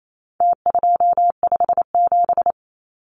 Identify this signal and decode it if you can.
Morse code, heard as a keyed tone, about 28 wpm, 700 Hz. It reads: T257
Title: Numbers - Contest Style - 4-Digit 28wpm